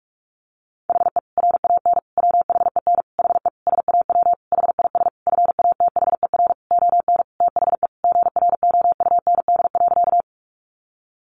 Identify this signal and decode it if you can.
Morse code, heard as a keyed tone, about 44 wpm, 700 Hz. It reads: HE RAN WHEN HE SAW HIS FATHER ON THE GROUND.
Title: Sentences from Top 500 Words 45wpm